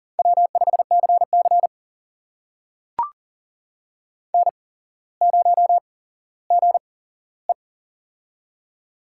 Morse code, heard as a keyed tone, about 40 wpm, spaced as for 17 wpm, 700 Hz. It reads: W5CC N0GE
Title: USA Call Signs - 4 Characters - 8x Character Spacing - 40wpm